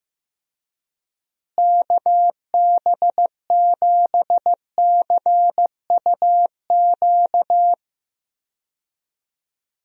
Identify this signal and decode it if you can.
Morse code, heard as a keyed tone, about 15 wpm, 700 Hz. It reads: KB7CUQ
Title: USA Call Signs - 6 Characters - Repeated 3x - 15wpm